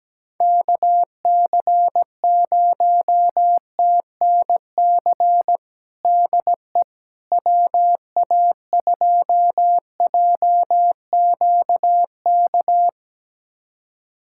Morse code, heard as a keyed tone, about 17 wpm, 700 Hz. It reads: KC0TNC DE WA2JQK
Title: USA Callsign DE Callsign - 6 Characters - 17wpm